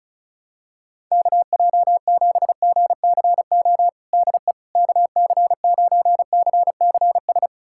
Morse code, heard as a keyed tone, about 36 wpm, 700 Hz. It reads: KJ7GCO DE KC9CCS
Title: USA Callsign DE Callsign 35wpm